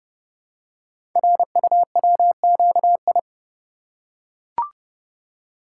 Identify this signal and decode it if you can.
Morse code, heard as a keyed tone, about 30 wpm, 700 Hz. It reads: RUWQI
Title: Code Groups - 5 Letters - 30wpm